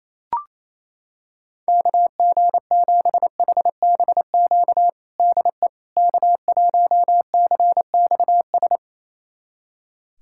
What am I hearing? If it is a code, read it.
Morse code, heard as a keyed tone, about 28 wpm, 700 Hz. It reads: KG7HBQ DE K1CXS
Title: USA Callsign DE Callsign 28wpm